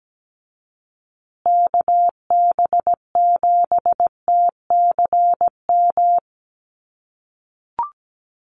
Morse code, heard as a keyed tone, about 17 wpm, 700 Hz. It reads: KB7TCM
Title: USA Call Signs Repeated 2x 17wpm